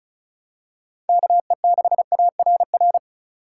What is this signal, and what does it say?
Morse code, heard as a keyed tone, about 36 wpm, 700 Hz. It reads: KE6ARR
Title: USA Call Signs - 6 Characters - 4x Character Spacing - 35wpm